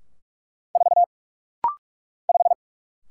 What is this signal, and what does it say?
Morse code, heard as a keyed tone, about 44 wpm, 700 Hz. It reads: 4 5
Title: Single Letter-Number Alt - Rapid-Fire 45wpm